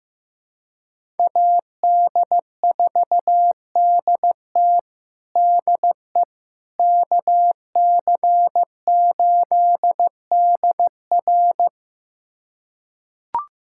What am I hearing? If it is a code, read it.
Morse code, heard as a keyed tone, about 16 wpm, 700 Hz. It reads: AD4DT DE KC8DR
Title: USA Callsign DE Callsign - 5 Characters - 15wpm